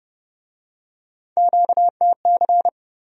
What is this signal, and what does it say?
Morse code, heard as a keyed tone, about 28 wpm, 700 Hz. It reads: QTC